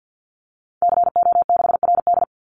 Morse code, heard as a keyed tone, about 50 wpm, 700 Hz. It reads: CO6RD
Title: Call Signs - 50wpm